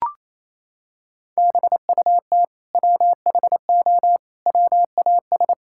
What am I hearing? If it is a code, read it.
Morse code, heard as a keyed tone, about 28 wpm, 700 Hz. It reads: BUT WHO WAS
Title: Sentences from Top 1000 Words - Review Part 5 - 28wpm